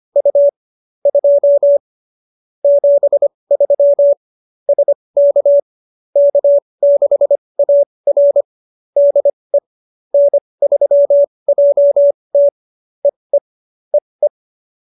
Morse code, heard as a keyed tone, about 25 wpm, spaced as for 20 wpm, 570 Hz. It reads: U 2 73 SK K6AR DE N3JT EE EE
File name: QSO307 20